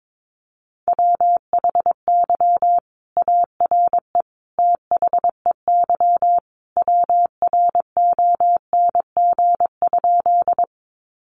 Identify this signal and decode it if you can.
Morse code, heard as a keyed tone, about 22 wpm, 700 Hz. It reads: WHY ARE THEY WRONG?
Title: Sentences from Top 900 Words 22wpm